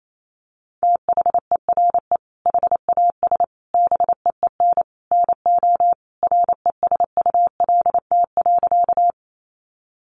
Morse code, heard as a keyed tone, about 28 wpm, 700 Hz. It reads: THERE HAS BEEN NO RESULT.
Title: Sentences from Top 700 Words 28wpm